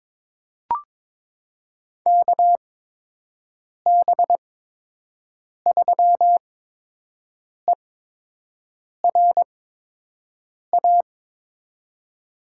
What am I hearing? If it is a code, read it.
Morse code, heard as a keyed tone, about 23 wpm, spaced as for 6 wpm, 700 Hz. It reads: KB3ERA
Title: USA Call Signs - 6 Characters - 8x Character Spacing - 22wpm